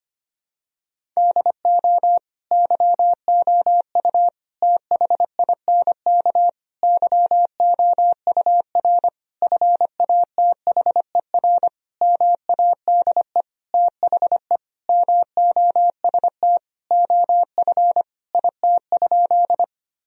Morse code, heard as a keyed tone, about 25 wpm, 700 Hz. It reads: DO YOU THINK YOUR FATHER MADE THE MOST OF IT?
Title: Sentences from Top 1000 Words - Review Part 1 - 25wpm